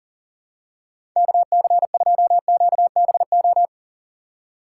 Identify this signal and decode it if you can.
Morse code, heard as a keyed tone, about 38 wpm, 700 Hz. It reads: KC2QBO
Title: USA Call Signs - 6 Characters - 4x Character Spacing - 40wpm